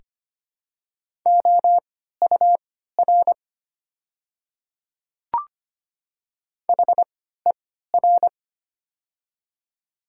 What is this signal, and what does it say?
Morse code, heard as a keyed tone, about 25 wpm, spaced as for 14 wpm, 700 Hz. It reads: OUR HER
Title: 3x Character Spacing - Top 500 Words - 25wpm